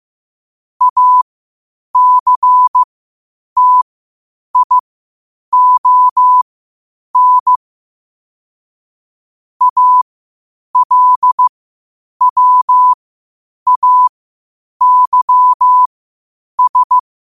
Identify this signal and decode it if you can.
Morse code, heard as a keyed tone, about 15 wpm, spaced as for 8 wpm, 1000 Hz. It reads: ACTION ALWAYS
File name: qotdfiles / 8wpm